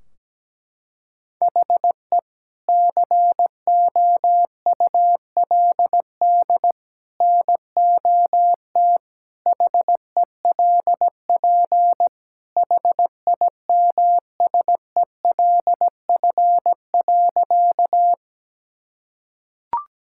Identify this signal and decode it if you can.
Morse code, heard as a keyed tone, about 17 wpm, 700 Hz. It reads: HE COULD NOT HELP HIMSELF.
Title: Sentences from Top 1600 Words 17wpm